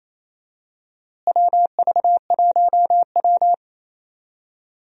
Morse code, heard as a keyed tone, about 28 wpm, 700 Hz. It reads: WV1W
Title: USA Call Signs - 4 Characters - Repeated 3x - 28wpm